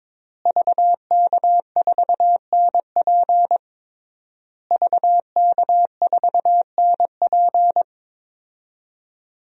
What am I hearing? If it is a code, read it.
Morse code, heard as a keyed tone, about 22 wpm, 700 Hz. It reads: VK4NP VK4NP
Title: DXCC-100 Call Signs - 5 Characters - Repeated 3x - 22wpm